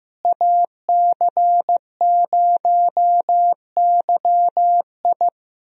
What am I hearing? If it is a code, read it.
Morse code, heard as a keyed tone, about 16 wpm, 700 Hz. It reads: AC0YI
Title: USA Call Signs - 5 Characters - Repeated 2x - 15wpm